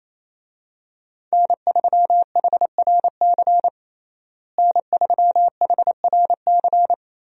Morse code, heard as a keyed tone, about 28 wpm, 700 Hz. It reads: N3HRC N3HRC
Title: USA Call Signs - 5 Characters - Repeated 3x - 28wpm